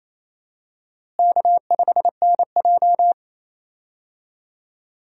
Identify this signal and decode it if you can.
Morse code, heard as a keyed tone, about 28 wpm, 700 Hz. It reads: K5NJ
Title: USA Call Signs - 4 Characters - Repeated 3x - 28wpm